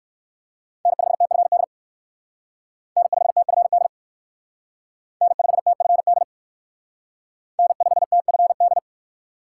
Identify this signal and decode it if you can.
Morse code, heard as a keyed tone, about 58 wpm, 700 Hz. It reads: N5TFD N5TFD N5TFD N5TFD